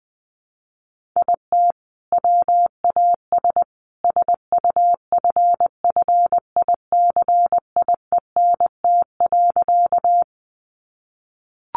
Morse code, heard as a keyed tone, about 20 wpm, 700 Hz. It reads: IT WAS SUFFICIENT.
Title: Sentences from Top 1400 Words 20wpm